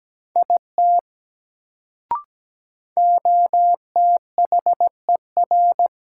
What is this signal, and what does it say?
Morse code, heard as a keyed tone, about 17 wpm, 700 Hz. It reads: IT OTHER
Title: Sentences from Top 100 Words - New Word Drill - 17wpm